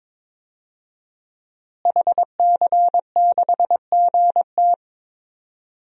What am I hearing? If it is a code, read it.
Morse code, heard as a keyed tone, about 22 wpm, 700 Hz. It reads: HC6GT
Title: DXCC-100 Call Signs - 5 Characters - Repeated 3x - 22wpm